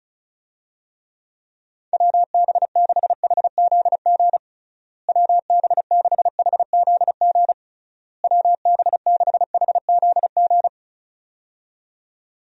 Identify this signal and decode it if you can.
Morse code, heard as a keyed tone, about 36 wpm, 700 Hz. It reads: WB6HZG WB6HZG WB6HZG
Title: USA Call Signs - 6 Characters - Repeated 3x - 35wpm